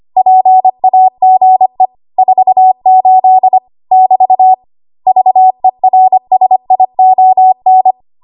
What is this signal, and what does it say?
Morse code, heard as a keyed tone, about 25 wpm, 750 Hz. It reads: PAGE 48 <BT> VERSION
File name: Archive / 25 WPM / 25WPM